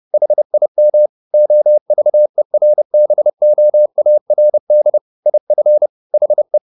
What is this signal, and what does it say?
Morse code, heard as a keyed tone, about 30 wpm, 600 Hz. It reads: HIM OVERBOARD IF HE
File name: the old man and the sea / the old man and the sea 1 30wpm